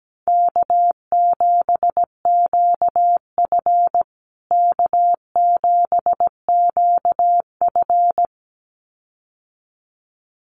Morse code, heard as a keyed tone, about 17 wpm, 700 Hz. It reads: K7QF K7QF
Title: USA Call Signs Repeated 2x 17wpm